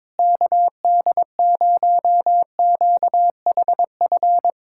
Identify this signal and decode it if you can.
Morse code, heard as a keyed tone, about 22 wpm, 700 Hz. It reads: KD0QHF
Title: USA Callsign DE Callsign - 6 Characters - 22wpm